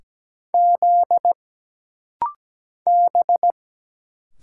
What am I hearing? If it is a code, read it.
Morse code, heard as a keyed tone, about 17 wpm, 700 Hz. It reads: Z B
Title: Single Letter-Number - Rapid-Fire 17wpm